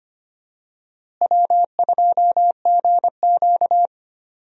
Morse code, heard as a keyed tone, about 25 wpm, 700 Hz. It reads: W2GQ
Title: USA Call Signs - 4 Characters - Repeated 3x - 25wpm